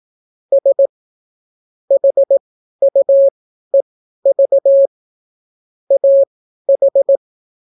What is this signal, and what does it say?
Morse code, heard as a keyed tone, about 18 wpm, spaced as for 12 wpm, 550 Hz. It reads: S HUEV AH